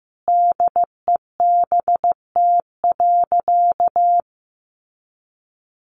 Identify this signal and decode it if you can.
Morse code, heard as a keyed tone, about 15 wpm, 700 Hz. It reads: DEBT.
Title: Sentences from Top 1300 Words 15wpm